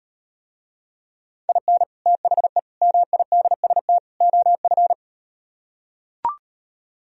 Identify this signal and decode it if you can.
Morse code, heard as a keyed tone, about 38 wpm, 700 Hz. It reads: IN THE MIDST OF